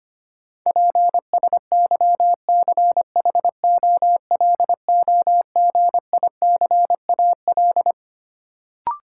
Morse code, heard as a keyed tone, about 25 wpm, 700 Hz. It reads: PSYCHOLOGICAL
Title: Sentences from Top 1400 Words - New Word Drill Alt - 25wpm